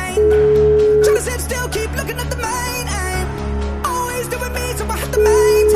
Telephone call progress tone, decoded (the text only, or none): ringback tone